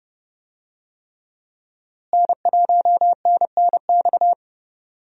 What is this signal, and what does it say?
Morse code, heard as a keyed tone, about 30 wpm, 700 Hz. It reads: N1NNX
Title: USA Call Signs - 5 Characters - Repeated 3x - 30wpm